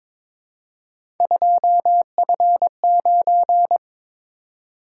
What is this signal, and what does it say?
Morse code, heard as a keyed tone, about 22 wpm, 700 Hz. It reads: 2F9
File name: practice / 22wpm / Three Letter-Number Combinations - 22wpm